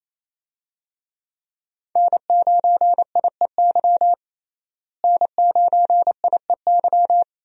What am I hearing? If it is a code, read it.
Morse code, heard as a keyed tone, about 28 wpm, 700 Hz. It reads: N9IEY N9IEY
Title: USA Call Signs - 5 Characters - Repeated 2x - 28wpm